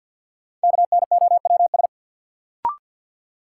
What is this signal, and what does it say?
Morse code, heard as a keyed tone, about 46 wpm, 700 Hz. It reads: KNOWS
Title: Sentences from Top 1000 Words - Review Part 1 - New Word Drill - 50wpm